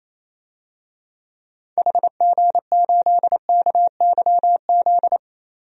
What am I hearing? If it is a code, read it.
Morse code, heard as a keyed tone, about 28 wpm, 700 Hz. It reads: HG8KYZ
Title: DXCC-100 Call Signs - 6 Characters - Repeated 2x - 28wpm